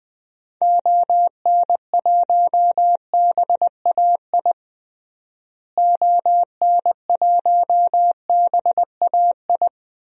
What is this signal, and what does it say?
Morse code, heard as a keyed tone, about 20 wpm, 700 Hz. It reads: ON1BAI ON1BAI
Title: DXCC-100 Call Signs - 6 Characters - Repeated 2x - 20wpm